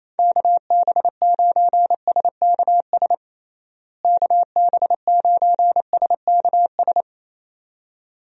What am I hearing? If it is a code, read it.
Morse code, heard as a keyed tone, about 27 wpm, 700 Hz. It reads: KB9SKS KB9SKS
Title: USA Call Signs - 6 Characters - Repeated 2x - 28wpm